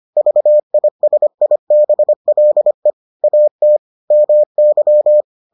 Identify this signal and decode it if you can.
Morse code, heard as a keyed tone, about 25 wpm, 600 Hz. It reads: VISIBLE AT MY